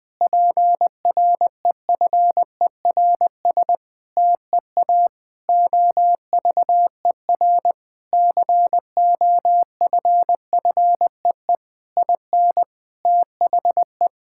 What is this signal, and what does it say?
Morse code, heard as a keyed tone, about 21 wpm, 700 Hz. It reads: PREFERS TEA OVER COFFEE IN THE
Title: Difficult Sentences with Hints - 20wpm